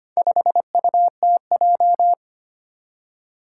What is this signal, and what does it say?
Morse code, heard as a keyed tone, about 25 wpm, 700 Hz. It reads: 5UTJ
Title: Four Challenging Letter-Number Combinations 25wpm